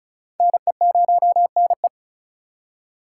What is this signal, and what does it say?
Morse code, heard as a keyed tone, about 46 wpm, 700 Hz. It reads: NE0NE